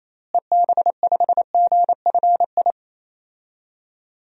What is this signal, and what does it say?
Morse code, heard as a keyed tone, about 28 wpm, 700 Hz. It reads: EB5GFI
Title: DXCC-100 Call Signs - 6 Characters - Repeated 2x - 28wpm